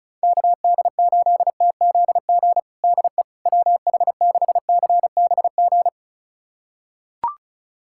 Morse code, heard as a keyed tone, about 36 wpm, 700 Hz. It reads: KD8TZG DE WH6CBG
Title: USA Callsign DE Callsign - 6 Characters - 35wpm